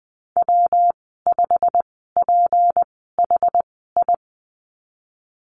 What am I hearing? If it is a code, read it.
Morse code, heard as a keyed tone, about 20 wpm, spaced as for 15 wpm, 700 Hz. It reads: W5PHI
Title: 161 Lesson 26g - Letter g - Call Signs 20wpm